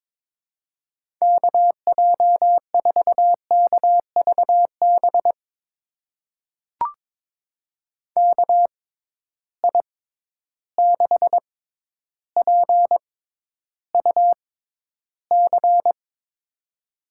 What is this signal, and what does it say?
Morse code, heard as a keyed tone, about 22 wpm, spaced as for 11 wpm, 700 Hz. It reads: KJ4KVB KI6PUC
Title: USA Call Signs - 6 Characters - 6x Character Spacing - 22wpm